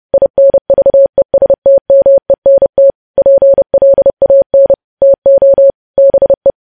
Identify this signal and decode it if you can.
Morse code, heard as a keyed tone, about 30 wpm, 550 Hz. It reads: INVESTMENT PLAN TO BE
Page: news-headlines-in-morse-code-at-30-wpm